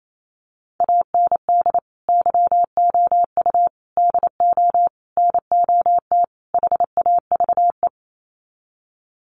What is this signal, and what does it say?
Morse code, heard as a keyed tone, about 28 wpm, 700 Hz. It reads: AND YOU DO NOT HAVE
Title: Sets of 5 Words Encore 28wpm